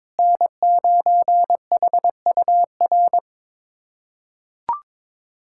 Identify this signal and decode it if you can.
Morse code, heard as a keyed tone, about 21 wpm, 700 Hz. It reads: N9HUR
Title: USA Callsign DE Callsign - 5 Characters - 22wpm